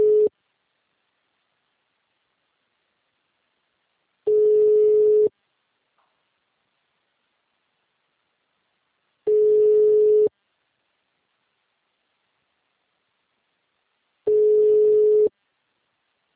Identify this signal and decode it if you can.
Ringback tone (Europe, 425 Hz).
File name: out- -2002- - - .148